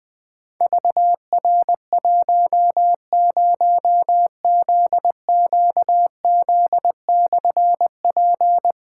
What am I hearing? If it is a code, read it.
Morse code, heard as a keyed tone, about 20 wpm, 700 Hz. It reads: VR10ZQZ/P
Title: Difficult Call Signs 20wpm